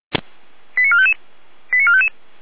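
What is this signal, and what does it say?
Five-tone selective call (ZVEI), heard as quick runs of five tones, tones of about 70 ms. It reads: 89455 89455
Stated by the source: ZVEI five-tone selective call received by SDR, Feuerwehr Orlamünde fire brigade alerting.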